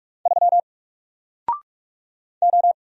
Morse code, heard as a keyed tone, about 44 wpm, 700 Hz. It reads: 3 O
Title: Single Letter-Number Alt 45wpm